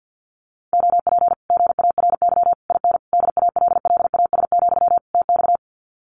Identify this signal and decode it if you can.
Morse code, heard as a keyed tone, about 50 wpm, 700 Hz. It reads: OP GARY IN DALLAS, TX
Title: Common QSO Phrases Encore 50wpm